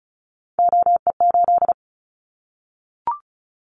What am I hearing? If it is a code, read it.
Morse code, heard as a keyed tone, about 36 wpm, 700 Hz. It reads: OE8
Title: Callsign Prefixes for Top 100 DXCC - 35wpm